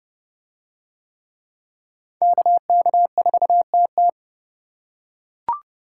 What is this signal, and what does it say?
Morse code, heard as a keyed tone, about 28 wpm, 700 Hz. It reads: KK4TT